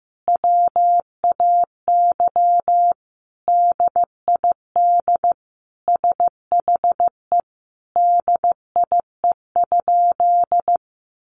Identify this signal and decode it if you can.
Morse code, heard as a keyed tone, about 14 wpm, 700 Hz. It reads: WAY DID SHE DIE?
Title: Sentences from Top 1300 Words 15wpm